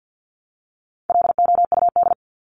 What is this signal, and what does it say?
Morse code, heard as a keyed tone, about 50 wpm, 700 Hz. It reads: LOUD
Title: Top 1751-2000 Words 50wpm